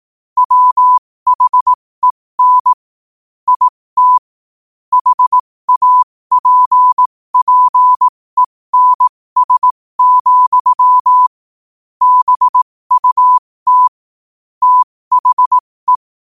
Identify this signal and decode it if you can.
Morse code, heard as a keyed tone, about 18 wpm, spaced as for 15 wpm, 1000 Hz. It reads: WHEN IT HAPPENS, BUT THE
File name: qotdfiles / 15wpm